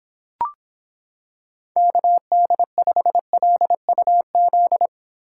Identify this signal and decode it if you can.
Morse code, heard as a keyed tone, about 26 wpm, 700 Hz. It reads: KD5LUZ